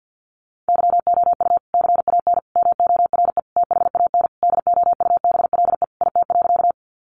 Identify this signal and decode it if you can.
Morse code, heard as a keyed tone, about 50 wpm, 700 Hz. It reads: YOU CAN MORE THAN DOUBLE IT.
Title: Sentences from Top 1400 Words 50wpm